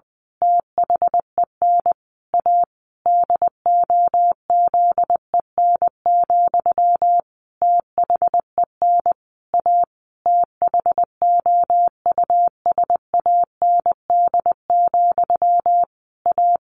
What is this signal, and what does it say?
Morse code, heard as a keyed tone, about 20 wpm, 700 Hz. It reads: THEN A DOZEN, THEN A THOUSAND, A
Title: A Sound Of Thunder 20wpm